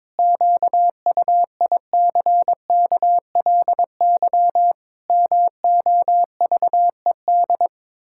Morse code, heard as a keyed tone, about 22 wpm, 700 Hz. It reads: QUICKLY MOVED
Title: Pangram Challenge v2 - 22wpm